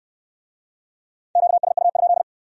Morse code, heard as a keyed tone, about 65 wpm, 700 Hz. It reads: QSU?